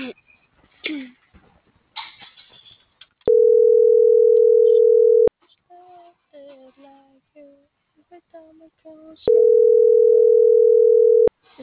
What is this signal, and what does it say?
Ringback tone (North America), 440 + 480 Hz, with speech.
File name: CENTRAL TERRESTRE / out els..laz.elo.mo.24